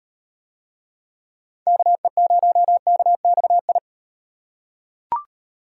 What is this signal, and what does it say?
Morse code, heard as a keyed tone, about 35 wpm, 700 Hz. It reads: KE0KXI